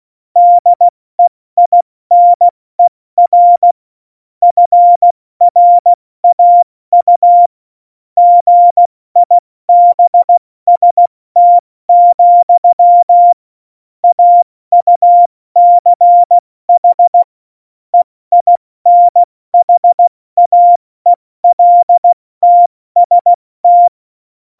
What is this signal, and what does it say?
Morse code, heard as a keyed tone, about 16 wpm, 700 Hz. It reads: DEINER FRAU GIBST, AUCH EINHAELTST
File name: Morsekurs / Witze / Witz 11. Juni 2019, 70 F80